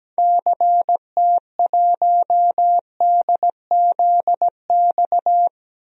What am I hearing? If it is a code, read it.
Morse code, heard as a keyed tone, about 17 wpm, 700 Hz. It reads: CT1DZX
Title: DXCC-100 Call Signs - 6 Characters - 17wpm